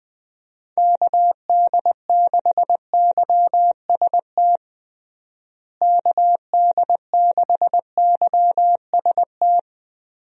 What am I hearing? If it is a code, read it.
Morse code, heard as a keyed tone, about 20 wpm, 700 Hz. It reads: KD6YST KD6YST
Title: USA Call Signs - 6 Characters - Repeated 3x - 20wpm